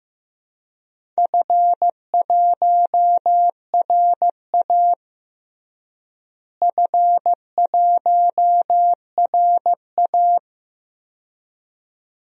Morse code, heard as a keyed tone, about 15 wpm, 700 Hz. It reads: F1RA F1RA
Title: DXCC-100 Call Signs - 4 Characters - Repeated 2x - 15wpm